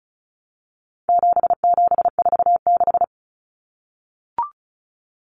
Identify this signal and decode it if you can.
Morse code, heard as a keyed tone, about 34 wpm, 700 Hz. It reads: 7746
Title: Numbers - 4-Digit 35wpm